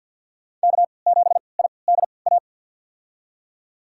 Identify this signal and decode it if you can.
Morse code, heard as a keyed tone, about 52 wpm, spaced as for 28 wpm, 700 Hz. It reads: K7IDA
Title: USA Call Signs - 3x Character Spacing - 50wpm